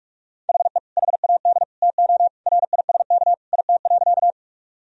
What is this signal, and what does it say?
Morse code, heard as a keyed tone, about 46 wpm, 700 Hz. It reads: HE HAD TO RISK IT.
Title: Sentences from Top 1000 Words - Review Part 4 - 45wpm